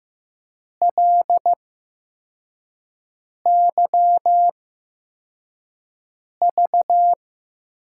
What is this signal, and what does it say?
Morse code, heard as a keyed tone, about 15 wpm, spaced as for 4 wpm, 700 Hz. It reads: LYV
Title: USA Call Signs - 6 Characters - 8x Character Spacing - 15wpm